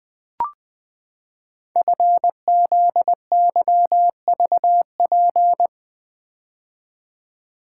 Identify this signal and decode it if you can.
Morse code, heard as a keyed tone, about 20 wpm, 700 Hz. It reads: FZYVP
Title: Code Groups - 5 Letter:Numbers - 20wpm